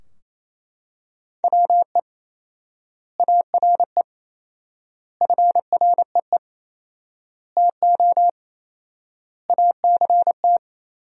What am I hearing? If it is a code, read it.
Morse code, heard as a keyed tone, about 28 wpm, 700 Hz. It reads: WE ARE FREE TO ACT
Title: 4x Word-Spacing - Sentences from Top 500 Words - 28wpm